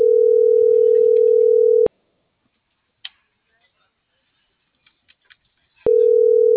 Ringback tone (North America), 440 + 480 Hz.